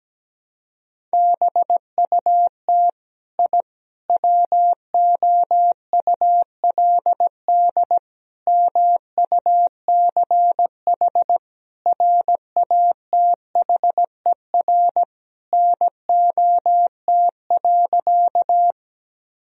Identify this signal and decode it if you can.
Morse code, heard as a keyed tone, about 17 wpm, 700 Hz. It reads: BUT I WOULD MUCH RATHER NOT.
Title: Sentences from Top 1000 Words - Review Part 4 - 17wpm